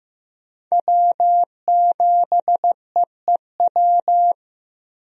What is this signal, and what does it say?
Morse code, heard as a keyed tone, about 15 wpm, 700 Hz. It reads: W7EEW
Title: USA Call Signs - 5 Characters - Repeated 2x - 15wpm